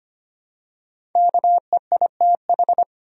Morse code, heard as a keyed tone, about 25 wpm, 700 Hz. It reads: KEITH
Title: USA Common Names - 25wpm